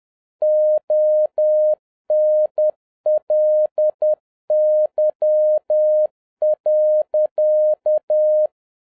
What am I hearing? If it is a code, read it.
Morse code, heard as a keyed tone, about 10 wpm, 610 Hz. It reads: ONLY.